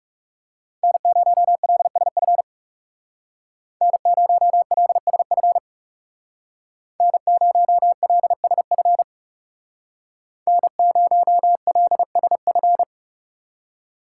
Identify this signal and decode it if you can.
Morse code, heard as a keyed tone, about 37 wpm, 700 Hz. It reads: N0LSF N0LSF N0LSF N0LSF
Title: Speed Racing - USA 1x3 Call Signs 30wpm